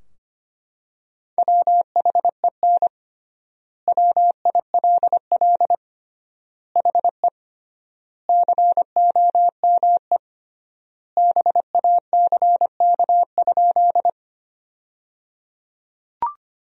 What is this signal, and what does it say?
Morse code, heard as a keyed tone, about 25 wpm, 700 Hz. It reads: WHEN WILL HE COME BACK?
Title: 3x Word-Spacing - Sentences from Top 200 Words - 25wpm